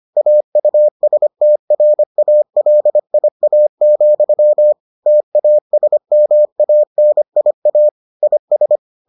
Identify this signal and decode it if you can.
Morse code, heard as a keyed tone, about 25 wpm, 600 Hz. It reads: AUSTRALIA, TASMANIA IS